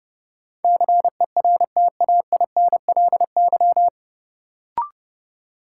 Morse code, heard as a keyed tone, about 31 wpm, 700 Hz. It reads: CERTAINLY
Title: Sentences from Top 900 Words - New Word Drill - 30wpm